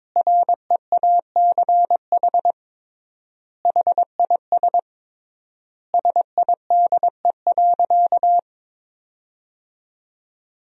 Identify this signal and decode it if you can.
Morse code, heard as a keyed tone, about 22 wpm, 700 Hz. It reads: REACH HIS SIDE.
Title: 3x Word-Spacing - Sentences from Top 400 Words - 22wpm